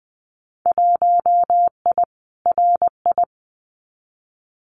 Morse code, heard as a keyed tone, about 19 wpm, 700 Hz. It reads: 1I RI
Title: Winter Field Day Exchanges Encore 20wpm